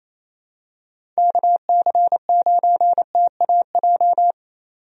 Morse code, heard as a keyed tone, about 28 wpm, 700 Hz. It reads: KC9TAJ
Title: USA Call Signs - 6 Characters - 28wpm